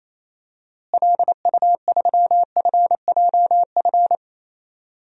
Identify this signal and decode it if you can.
Morse code, heard as a keyed tone, about 27 wpm, 700 Hz. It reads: LU3FJF
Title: DXCC-100 Call Signs - 6 Characters - Repeated 2x - 28wpm